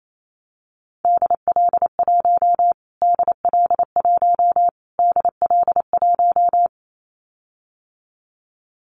Morse code, heard as a keyed tone, about 28 wpm, 700 Hz. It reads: DL1 DL1 DL1
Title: Callsign Prefixes for Top 25 DXCC - Repeated 3x - 28wpm